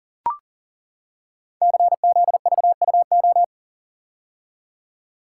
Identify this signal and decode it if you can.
Morse code, heard as a keyed tone, about 42 wpm, 700 Hz. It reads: CZVUO
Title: Code Groups - 5 Letter:Numbers - 40wpm